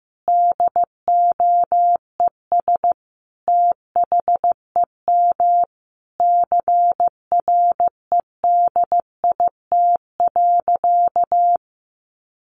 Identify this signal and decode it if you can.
Morse code, heard as a keyed tone, about 15 wpm, 700 Hz. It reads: DOES THEM CREDIT.
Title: Sentences from Top 1000 Words 15wpm